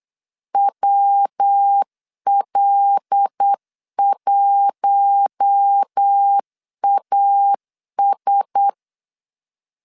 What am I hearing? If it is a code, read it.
Morse code, heard as a keyed tone, about 8 wpm, 800 Hz. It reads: WL1AS
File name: LEZIONI / 6 PKX0 / 6 P 8wpm 3025